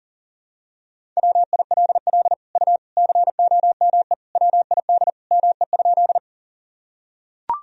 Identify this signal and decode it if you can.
Morse code, heard as a keyed tone, about 40 wpm, 700 Hz. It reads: WILL U COME WID ME?
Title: Sentences from Top 200 Words and Abbreviations 40wpm